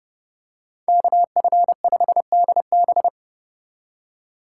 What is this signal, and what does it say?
Morse code, heard as a keyed tone, about 29 wpm, 700 Hz. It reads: KF5DB